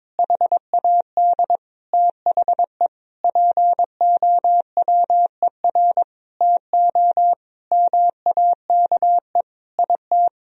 Morse code, heard as a keyed tone, about 22 wpm, 700 Hz. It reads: HAD THE POWER TO MAKE IT
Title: Sentences from Top 1000 Words - Review Part 3 - 22wpm